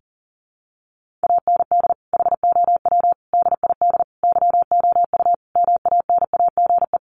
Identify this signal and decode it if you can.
Morse code, heard as a keyed tone, about 40 wpm, 700 Hz. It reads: AND HOW DID YOU MANAGE
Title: Sentences from Top 1400 Words 40wpm